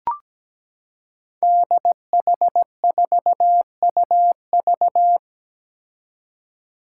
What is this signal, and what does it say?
Morse code, heard as a keyed tone, about 17 wpm, 700 Hz. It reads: DH4UV
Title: Five Challenging Letter-Number Combinations 17wpm